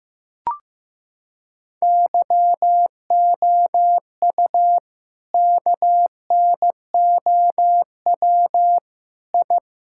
Morse code, heard as a keyed tone, about 15 wpm, 700 Hz. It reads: YOU KNOW I
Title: Sentences from Top 1000 Words - Review Part 6 - 15wpm